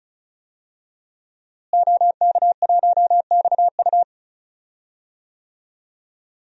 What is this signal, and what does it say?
Morse code, heard as a keyed tone, about 34 wpm, 700 Hz. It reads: OK1XU
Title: DXCC-100 Call Signs - 5 Characters - Repeated 2x - 35wpm